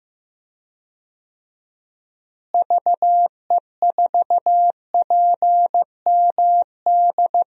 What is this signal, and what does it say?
Morse code, heard as a keyed tone, about 15 wpm, 700 Hz. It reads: VE4PMD
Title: DXCC-100 Call Signs - 6 Characters - Repeated 3x - 15wpm